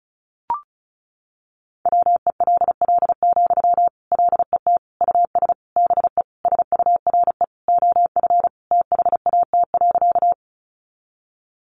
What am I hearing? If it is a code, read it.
Morse code, heard as a keyed tone, about 34 wpm, 700 Hz. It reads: WELL, LET US BE SURE OF THAT.
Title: Sentences from Top 300 Words 35wpm